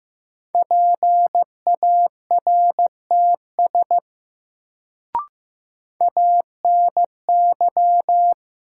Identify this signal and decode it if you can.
Morse code, heard as a keyed tone, about 15 wpm, 700 Hz. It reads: PARTS ANY
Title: Sets of 3 Words - New Word Drill - 15wpm